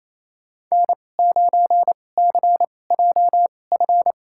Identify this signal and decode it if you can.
Morse code, heard as a keyed tone, about 28 wpm, spaced as for 20 wpm, 700 Hz. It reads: N9CJF
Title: USA Call Signs - 5 Characters - 2x Character Spacing - 28wpm